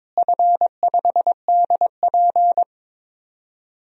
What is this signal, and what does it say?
Morse code, heard as a keyed tone, about 22 wpm, 700 Hz. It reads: F5DP
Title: DXCC-100 Call Signs - 4 Characters - Repeated 2x - 22wpm